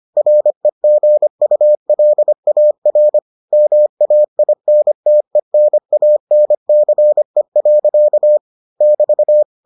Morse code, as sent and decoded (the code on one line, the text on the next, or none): .-. --. ..- .-.. .- .-. / -- .- .. -. - . -. .- -. -.-. .-.-.- / -...-
REGULAR MAINTENANCE. <BT>